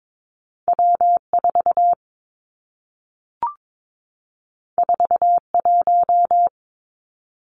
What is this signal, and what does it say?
Morse code, heard as a keyed tone, about 22 wpm, 700 Hz. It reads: W4 41
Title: Two Challenging Letter-Number Combinations 22wpm